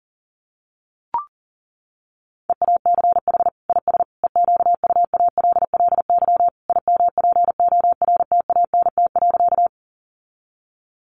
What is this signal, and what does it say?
Morse code, heard as a keyed tone, about 40 wpm, 700 Hz. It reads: EACH IS EQUALLY IMPORTANT.
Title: Sentences from Top 1400 Words 40wpm